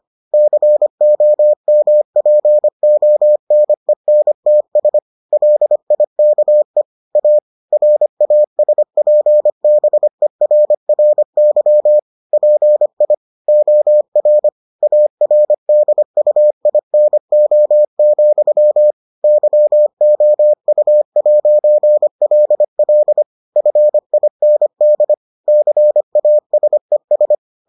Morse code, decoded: COMPONENTS LIKE A RASPBERRY PI OR ARDUINO, YOU'LL FIND CASES

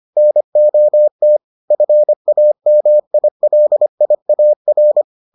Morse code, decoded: NOT FAMILIAR

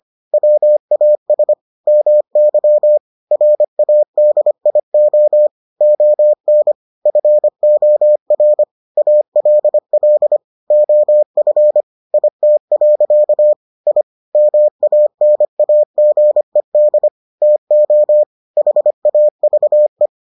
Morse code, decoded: WAS MY RADIO ON FOR ALL OF IT. I MANAGED TO HAVE

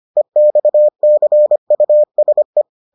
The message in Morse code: EXCUSE

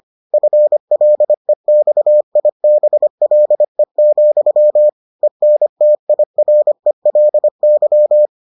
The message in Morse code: FLEXIBLE, ENTIRELY

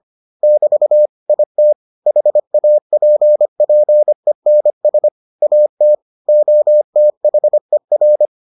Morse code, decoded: <BT> IT HAPPENS AT OTHER

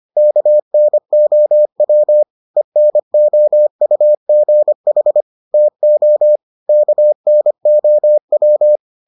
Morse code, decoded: KNOW ENOUGH TO KNOW

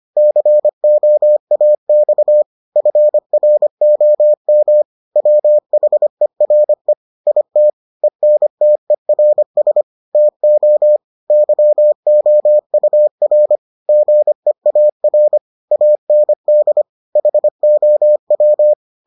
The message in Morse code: COAX FROM WHERE IT ENTERS TO YOUR GEAR AND HOW

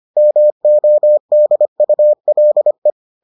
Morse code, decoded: MODULE

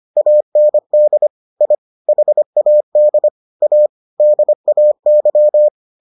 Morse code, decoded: AND I HAD A DAY

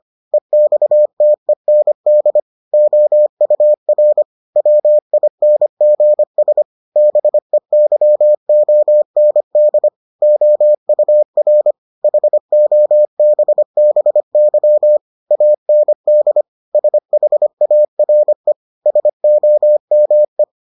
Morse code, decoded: EXTEND OUR WINGS BEYOND OUR HOBBY AND SHARE SOME

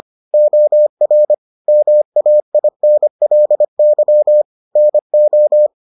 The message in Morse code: OR MAINLY NO

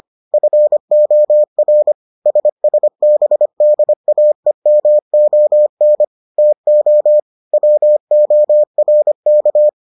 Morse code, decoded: FOR SSBDAEMON TO WORK

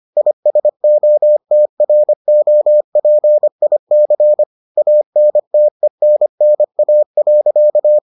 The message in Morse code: ISOTROPIC ANTENNA.

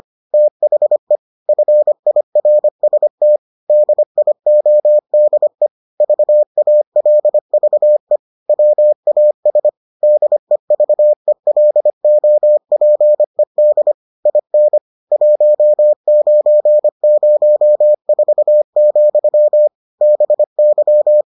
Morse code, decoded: THE FIRST DIODE VALVE WAS DEVELOPED IN 1904, BY

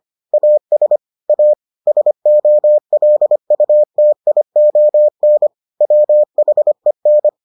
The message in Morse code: AS A SOLUTION WHEN

